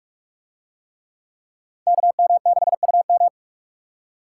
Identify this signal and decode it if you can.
Morse code, heard as a keyed tone, about 41 wpm, 700 Hz. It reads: KM6UM